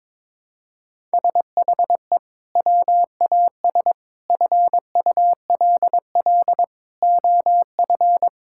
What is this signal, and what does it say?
Morse code, heard as a keyed tone, about 22 wpm, 700 Hz. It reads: SHE WAS FULL OF
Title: Sentences from Top 1000 Words - Review Part 6 - 22wpm